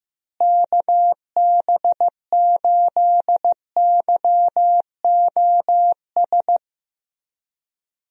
Morse code, heard as a keyed tone, about 15 wpm, 700 Hz. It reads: KB8YOS
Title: USA Call Signs - 6 Characters - Repeated 2x - 15wpm